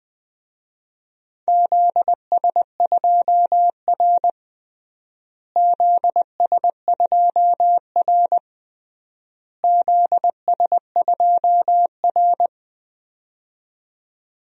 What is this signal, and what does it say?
Morse code, heard as a keyed tone, about 21 wpm, 700 Hz. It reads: ZS2R ZS2R ZS2R
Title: DXCC-100 Call Signs - 4 Characters - Repeated 3x - 20wpm